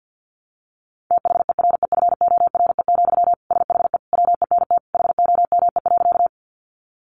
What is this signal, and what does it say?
Morse code, heard as a keyed tone, about 50 wpm, 700 Hz. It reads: THEREFORE, SHE WENT HOME.
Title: Sentences from Top 800 Words 50wpm